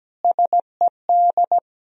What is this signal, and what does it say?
Morse code, heard as a keyed tone, about 17 wpm, 700 Hz. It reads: SED
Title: Sentences from Top 400 Words and Abbreviations 17wpm